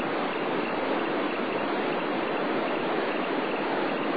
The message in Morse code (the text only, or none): none